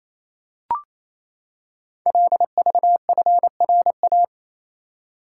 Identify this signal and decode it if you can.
Morse code, heard as a keyed tone, about 27 wpm, 700 Hz. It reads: LVFRA
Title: Code Groups - 5 Letters - 28wpm